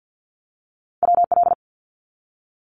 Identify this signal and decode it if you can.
Morse code, heard as a keyed tone, about 50 wpm, 700 Hz. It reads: WL
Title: Abbreviations 50wpm